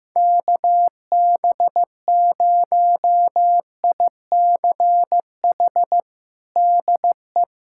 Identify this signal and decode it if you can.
Morse code, heard as a keyed tone, about 15 wpm, 700 Hz. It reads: KB0ICH DE
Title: USA Callsign DE Callsign - 6 Characters - 15wpm